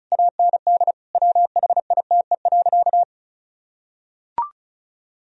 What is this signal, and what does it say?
Morse code, heard as a keyed tone, about 36 wpm, 700 Hz. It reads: AND WHITE.
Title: Sentences from Top 1000 Words - Review Part 2 - 35wpm